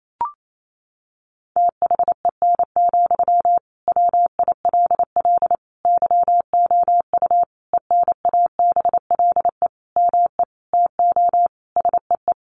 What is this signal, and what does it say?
Morse code, heard as a keyed tone, about 28 wpm, 700 Hz. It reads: THEN, WILL YOU ENABLE ME TO SEE
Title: Sentences from Top 1500 Words 28wpm